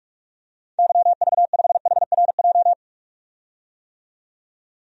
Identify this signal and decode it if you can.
Morse code, heard as a keyed tone, about 47 wpm, 700 Hz. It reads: YV5HRJ
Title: DXCC-100 Call Signs - 6 Characters - 45wpm